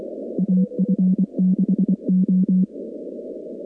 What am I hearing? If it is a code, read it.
Morse code, heard as a keyed tone, about 24 wpm, 190 Hz. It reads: AF6O
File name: contests / 2019-cw-wpx-kt5j-k5pi- -2330 0-19